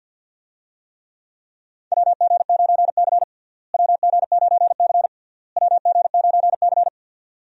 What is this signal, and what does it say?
Morse code, heard as a keyed tone, about 49 wpm, 700 Hz. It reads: WG9C WG9C WG9C
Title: USA Call Signs - 4 Characters - Repeated 3x - 50wpm